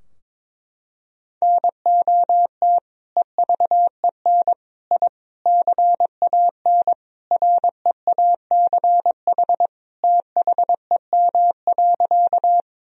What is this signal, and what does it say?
Morse code, heard as a keyed tone, about 21 wpm, 700 Hz. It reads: NOT EVEN I CAN REACH THEM.
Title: Sentences from Top 1000 Words - Review Part 2 - 22wpm